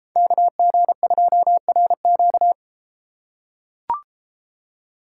Morse code, heard as a keyed tone, about 34 wpm, 700 Hz. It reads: KG2RQ